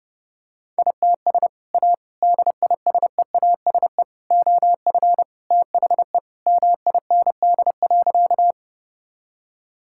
Morse code, heard as a keyed tone, about 32 wpm, 700 Hz. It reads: ITS A DISEASE OF THE MIND.
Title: Sentences from Top 1000 Words - Review Part 6 - 30wpm